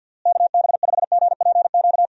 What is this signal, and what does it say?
Morse code, heard as a keyed tone, about 52 wpm, 700 Hz. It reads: KB5GPQ
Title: USA Call Signs - 6 Characters - Repeated 3x - 50wpm